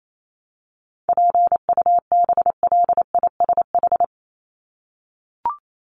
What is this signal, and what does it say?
Morse code, heard as a keyed tone, about 28 wpm, 700 Hz. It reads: PUBLISH
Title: ED Suffix - Top 100 Words with Base Words 28wpm